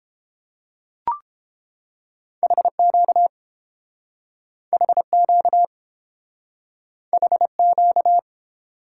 Morse code, heard as a keyed tone, about 30 wpm, 700 Hz. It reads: HQ HQ HQ